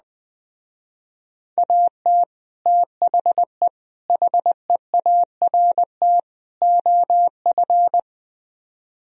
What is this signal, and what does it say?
Morse code, heard as a keyed tone, about 20 wpm, 700 Hz. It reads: AT THE HEART OF
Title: Speed-Racing - Sets of 4 Words 17wpm